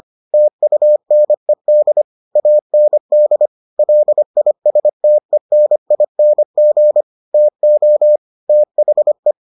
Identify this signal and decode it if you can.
Morse code, heard as a keyed tone, about 25 wpm, 600 Hz. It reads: TUNED AND LISTENING TO THE